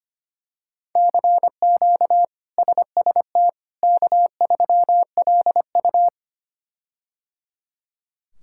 Morse code, heard as a keyed tone, about 25 wpm, 700 Hz. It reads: CQ SST K3LU
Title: K1USN SST QSO Practice - 25wpm